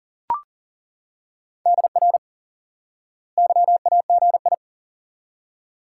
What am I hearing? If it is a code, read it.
Morse code, heard as a keyed tone, about 40 wpm, 700 Hz. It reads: DR YAGI
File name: IQE Course 3 - Lesson 89 - YAGI - Two Elements - 40wpm